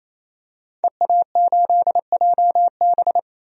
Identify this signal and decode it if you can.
Morse code, heard as a keyed tone, about 28 wpm, 700 Hz. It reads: EA8JB
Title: DXCC-100 Call Signs - 5 Characters - 28wpm